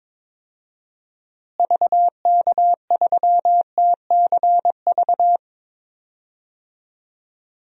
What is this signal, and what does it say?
Morse code, heard as a keyed tone, about 21 wpm, 700 Hz. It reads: VK3TCV
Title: DXCC-100 Call Signs - 6 Characters - Repeated 2x - 22wpm